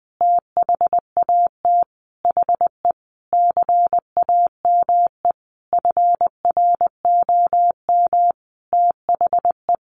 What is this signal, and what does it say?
Morse code, heard as a keyed tone, about 19 wpm, 700 Hz. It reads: THAT HE CAME FROM THE
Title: Sentences from Top 1500 Words 20wpm